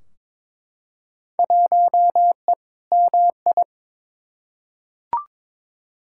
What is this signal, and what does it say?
Morse code, heard as a keyed tone, about 22 wpm, 700 Hz. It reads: 1E MI
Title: Field Day Exchanges - 22wpm